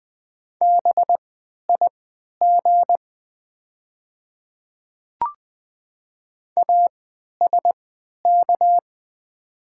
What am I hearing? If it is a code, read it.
Morse code, heard as a keyed tone, about 20 wpm, spaced as for 11 wpm, 700 Hz. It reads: BIG ASK
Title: 3x Character Spacing - Top 200 Words - 20wpm